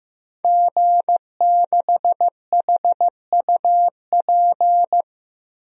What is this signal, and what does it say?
Morse code, heard as a keyed tone, about 15 wpm, 700 Hz. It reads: G6HUP
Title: DXCC-100 Call Signs - 5 Characters - Repeated 2x - 15wpm